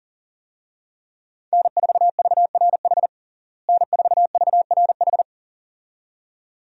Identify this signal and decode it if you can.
Morse code, heard as a keyed tone, about 40 wpm, 700 Hz. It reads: N4VRH N4VRH
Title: USA Call Signs - 5 Characters - Repeated 2x - 40wpm